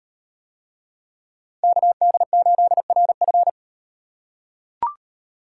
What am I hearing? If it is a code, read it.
Morse code, heard as a keyed tone, about 37 wpm, 700 Hz. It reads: KD8RF